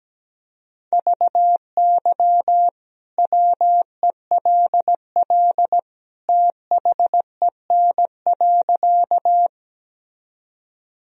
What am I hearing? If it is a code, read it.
Morse code, heard as a keyed tone, about 17 wpm, 700 Hz. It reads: VY WELL THEN.
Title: Sentences from Top 200 Words and Abbreviations 17wpm